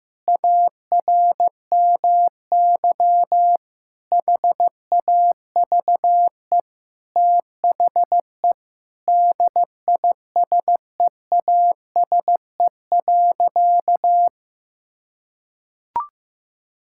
Morse code, heard as a keyed tone, about 16 wpm, 700 Hz. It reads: ARMY HAVE THE DISEASE.
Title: Sentences from Top 1000 Words - Review Part 6 - 15wpm